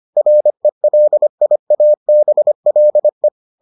Morse code, heard as a keyed tone, about 25 wpm, 600 Hz. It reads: RELIABLE